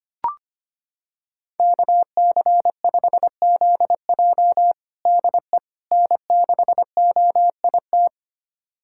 Morse code, heard as a keyed tone, about 25 wpm, 700 Hz. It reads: KC5ZJ DE N6OIT
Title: USA Callsign DE Callsign - 5 Characters - 25wpm